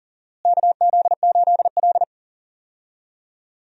Morse code, heard as a keyed tone, about 39 wpm, 700 Hz. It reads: KZ8L
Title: USA Call Signs - 4 Characters - Repeated 3x - 40wpm